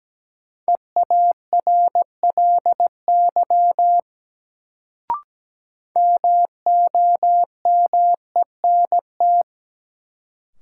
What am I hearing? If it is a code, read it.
Morse code, heard as a keyed tone, about 17 wpm, 700 Hz. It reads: EARLY MOMENT
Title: Sets of 3 Words - New Word Drill - 17wpm